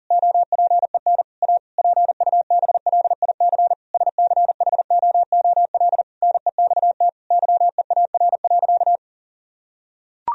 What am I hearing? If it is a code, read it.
Morse code, heard as a keyed tone, about 40 wpm, 700 Hz. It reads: OPEN A PUBLIC SCHOOL NEXT YEAR.
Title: Sentences from Top 1000 Words - Review Part 3 - 40wpm